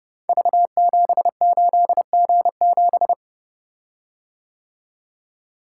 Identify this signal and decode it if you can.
Morse code, heard as a keyed tone, about 30 wpm, 700 Hz. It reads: V78G7
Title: Code Groups - 5 Letter:Numbers - 30wpm